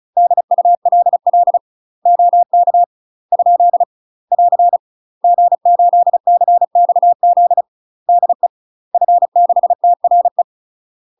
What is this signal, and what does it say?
Morse code, heard as a keyed tone, about 34 wpm, 700 Hz. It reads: DULL OK ? <AR> G8CXZ DE F6TRE
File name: UK MASTER QSOs TRAINING / 35WPM / UK MASTER QSOs 0003